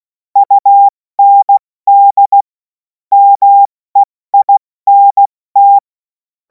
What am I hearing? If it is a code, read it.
Morse code, heard as a keyed tone, about 16 wpm, 800 Hz. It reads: UND MEINT
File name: Morsekurs / Witze / Witz 21. April 2026 70 F80